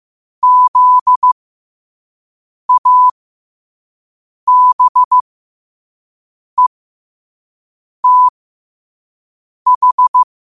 Morse code, heard as a keyed tone, about 15 wpm, spaced as for 5 wpm, 1000 Hz. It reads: ZABETH
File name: qotdfiles / 5wpm